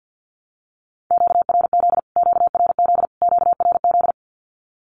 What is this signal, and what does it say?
Morse code, heard as a keyed tone, about 50 wpm, 700 Hz. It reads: QRZ QRZ QRZ
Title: Q-Codes Repeated 3x 50wpm